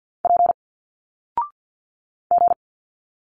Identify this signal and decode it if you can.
Morse code, heard as a keyed tone, about 50 wpm, 700 Hz. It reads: P G
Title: Single Letters 50wpm